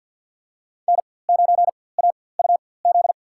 Morse code, heard as a keyed tone, about 52 wpm, spaced as for 24 wpm, 700 Hz. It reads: N9AUZ
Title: USA Call Signs - 5 Characters - 4x Character Spacing - 50wpm